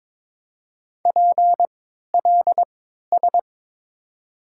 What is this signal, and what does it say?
Morse code, heard as a keyed tone, about 22 wpm, spaced as for 13 wpm, 700 Hz. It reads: PLS
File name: Top 90 QSO Elements - 3x Character Spacing - 22wpm